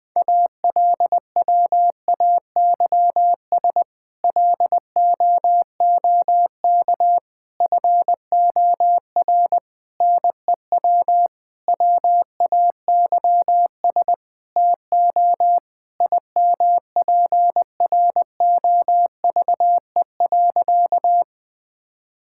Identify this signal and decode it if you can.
Morse code, heard as a keyed tone, about 20 wpm, 700 Hz. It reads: ALWAYS LOOK FOR NEW WAYS TO IMPROVE.
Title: Sentences with 6-Word Phrases - 20wpm